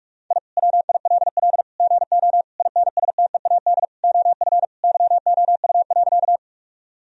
Morse code, heard as a keyed tone, about 46 wpm, 700 Hz. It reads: I WILL GO INSTEAD OF YOU.
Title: Sentences from Top 1000 Words - Review Part 4 - 45wpm